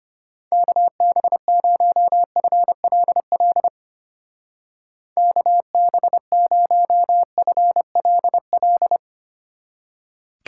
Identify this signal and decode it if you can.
Morse code, heard as a keyed tone, about 30 wpm, 700 Hz. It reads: KB0FLL KB0FLL
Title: Speed Racing - USA 2x3 Call Signs 25wpm